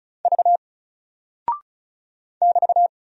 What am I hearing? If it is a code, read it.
Morse code, heard as a keyed tone, about 36 wpm, 700 Hz. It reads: V <BT>
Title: Single Character (Letter:Number:Punctuation:Prosign) 35wpm